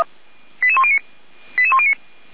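Five-tone selective call (ZVEI), heard as quick runs of five tones, tones of about 70 ms. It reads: 88109 88109
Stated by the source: ZVEI five-tone selective call received on SDR.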